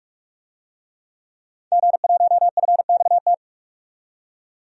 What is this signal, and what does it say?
Morse code, heard as a keyed tone, about 46 wpm, 700 Hz. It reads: G1FXT
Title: DXCC-100 Call Signs - 5 Characters - 45wpm